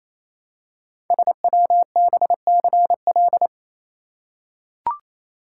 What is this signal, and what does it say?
Morse code, heard as a keyed tone, about 27 wpm, 700 Hz. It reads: SWBCL
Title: Code Groups - 5 Letters - 28wpm